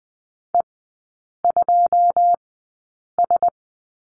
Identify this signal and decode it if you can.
Morse code, heard as a keyed tone, about 20 wpm, spaced as for 8 wpm, 700 Hz. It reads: E2S
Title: 100 Lesson 17e - Letter u - Three Characters 20wpm